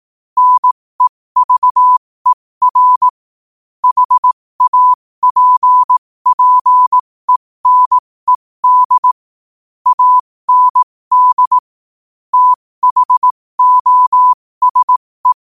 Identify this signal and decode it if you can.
Morse code, heard as a keyed tone, about 18 wpm, spaced as for 15 wpm, 1000 Hz. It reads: NEVER HAPPENED AND THOSE
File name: qotdfiles / 15wpm